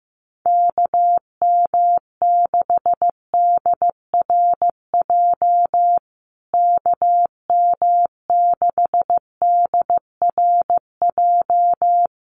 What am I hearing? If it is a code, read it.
Morse code, heard as a keyed tone, about 15 wpm, 700 Hz. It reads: KM6DRJ KM6DRJ
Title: USA Call Signs Repeated 3x 15wpm